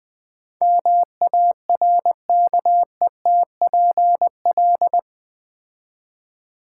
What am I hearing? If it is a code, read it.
Morse code, heard as a keyed tone, about 20 wpm, 700 Hz. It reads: MARKETPL
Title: Build A Word - 4 Letters At A Time - 11-Letter Words - 20wpm